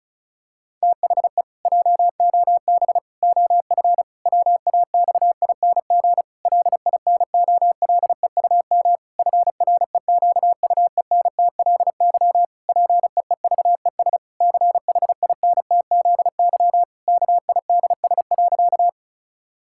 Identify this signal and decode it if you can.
Morse code, heard as a keyed tone, about 36 wpm, 700 Hz. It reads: THE JOB OF WAXING LINOLEUM FREQUENTLY PEEVES CHINTZY KIDS.
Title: Pangram Challenge v2 - 35wpm